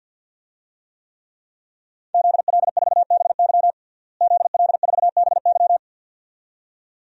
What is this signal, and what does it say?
Morse code, heard as a keyed tone, about 51 wpm, 700 Hz. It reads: ZL4BY ZL4BY
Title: DXCC-100 Call Signs - 5 Characters - Repeated 2x - 50wpm